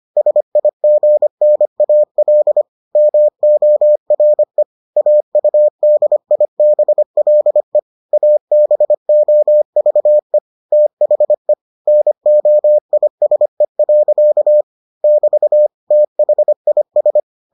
Morse code, heard as a keyed tone, about 25 wpm, 600 Hz. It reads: SIGNAL MORE AUDIBLE ABOVE THE NOISE. <BT> THIS